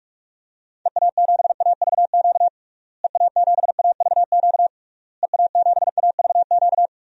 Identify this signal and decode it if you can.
Morse code, heard as a keyed tone, about 41 wpm, 700 Hz. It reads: EA7AVQ EA7AVQ EA7AVQ
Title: DXCC-100 Call Signs - 6 Characters - Repeated 3x - 45wpm